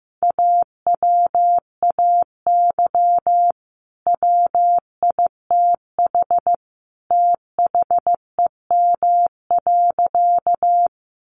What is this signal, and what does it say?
Morse code, heard as a keyed tone, about 15 wpm, 700 Hz. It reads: AWAY WITH THEM.
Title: Sentences from Top 600 Words 15wpm